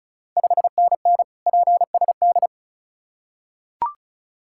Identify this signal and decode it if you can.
Morse code, heard as a keyed tone, about 36 wpm, 700 Hz. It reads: 5NN PSD
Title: Ontario QSO Party Exchange 35wpm